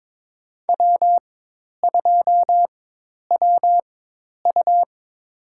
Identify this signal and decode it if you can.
Morse code, heard as a keyed tone, about 22 wpm, spaced as for 10 wpm, 700 Hz. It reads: W2WU
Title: USA Call Signs - 4 Characters - 4x Character Spacing - 22wpm